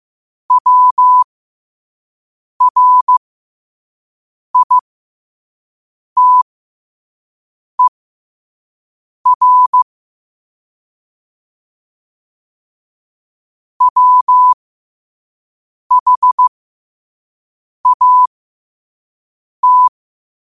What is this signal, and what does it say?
Morse code, heard as a keyed tone, about 15 wpm, spaced as for 5 wpm, 1000 Hz. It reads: WRITER WHAT
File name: qotdfiles / 5wpm